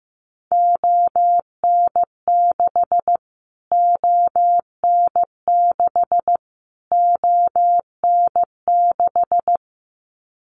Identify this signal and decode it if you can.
Morse code, heard as a keyed tone, about 15 wpm, 700 Hz. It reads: ON6 ON6 ON6
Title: Callsign Prefixes for Top 25 DXCC - Repeated 3x - 15wpm